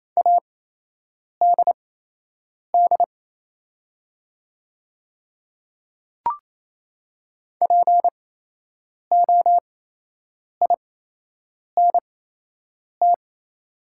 Morse code, heard as a keyed tone, about 27 wpm, spaced as for 8 wpm, 690 Hz. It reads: ADD POINT
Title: 8x Character Spacing - Top 200 Words - 28wpm